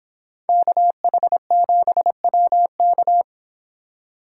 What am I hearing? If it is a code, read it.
Morse code, heard as a keyed tone, about 29 wpm, 700 Hz. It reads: KH7WK